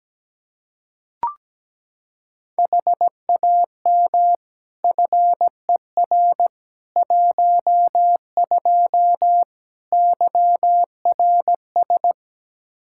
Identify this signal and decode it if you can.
Morse code, heard as a keyed tone, about 17 wpm, 700 Hz. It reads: HAM FER 12 YRS
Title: Practice QSO - Exchange 3 - More Info - 17wpm